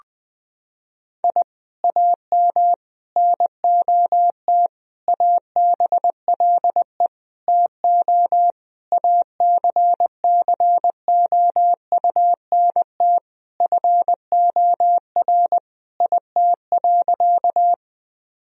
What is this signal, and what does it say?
Morse code, heard as a keyed tone, about 20 wpm, 700 Hz. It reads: I AM NOT ABLE TO ACCOUNT FOR IT.
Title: Sentences from Top 1000 Words - Review Part 2 - 20wpm